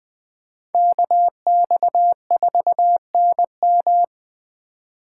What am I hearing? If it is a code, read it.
Morse code, heard as a keyed tone, about 20 wpm, 700 Hz. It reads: KX4NM
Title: K1USN SST QSO Practice - 20wpm